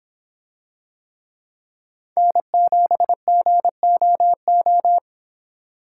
Morse code, heard as a keyed tone, about 26 wpm, 700 Hz. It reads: N7GOO